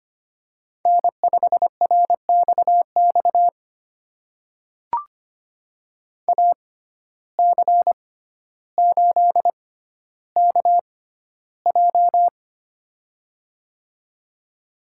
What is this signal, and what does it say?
Morse code, heard as a keyed tone, about 25 wpm, spaced as for 13 wpm, 700 Hz. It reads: N5RXX AC8KJ
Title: USA Call Signs - 5 Characters - 6x Character Spacing - 25wpm